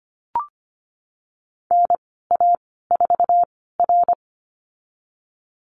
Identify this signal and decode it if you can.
Morse code, heard as a keyed tone, about 25 wpm, spaced as for 16 wpm, 700 Hz. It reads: NA4R
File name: Lesson 15g - Number 5 - Call Signs 25wpm